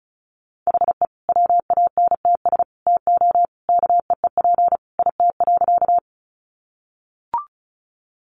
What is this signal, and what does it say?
Morse code, heard as a keyed tone, about 34 wpm, 700 Hz. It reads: HE WANTS TO KEEP IT.
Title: Additional Easy Phrases 35wpm